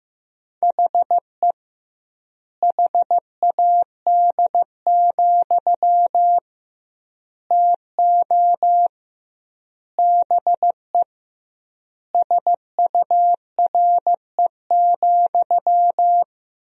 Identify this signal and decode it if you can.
Morse code, heard as a keyed tone, about 15 wpm, 700 Hz. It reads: HE HAD, TO BE SURE,
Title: 2x Word-Spacing - Sentences from Top 500 Words - 15wpm